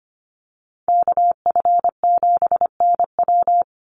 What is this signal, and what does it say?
Morse code, heard as a keyed tone, about 25 wpm, 700 Hz. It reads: KF7NW
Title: USA Call Signs 25wpm